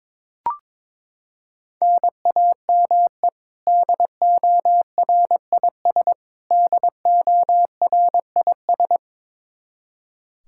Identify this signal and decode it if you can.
Morse code, heard as a keyed tone, about 22 wpm, 700 Hz. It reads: NAME DORIS DORIS
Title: Practice QSO - Exchange 1 - Beginning - 22wpm